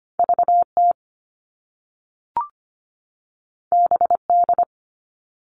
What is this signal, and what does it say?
Morse code, heard as a keyed tone, about 25 wpm, 700 Hz. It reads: VT BD
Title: All Cut Numbers - 2-Digit 25wpm